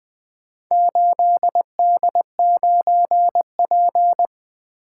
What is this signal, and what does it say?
Morse code, heard as a keyed tone, about 20 wpm, 700 Hz. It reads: 8D9P
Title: Four Letter-Number Combinations - 20wpm